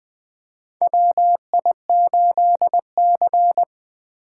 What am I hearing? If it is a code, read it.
Morse code, heard as a keyed tone, about 20 wpm, 700 Hz. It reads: WI8C
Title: USA Call Signs - 4 Characters - 4x Character Spacing - 20wpm